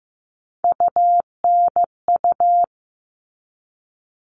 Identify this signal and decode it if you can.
Morse code, heard as a keyed tone, about 15 wpm, 700 Hz. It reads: UNU
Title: All Cut Numbers - 3-Digit 15wpm